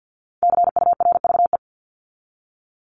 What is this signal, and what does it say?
Morse code, heard as a keyed tone, about 50 wpm, 700 Hz. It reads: CURVE
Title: Top 1501-1750 Words 50wpm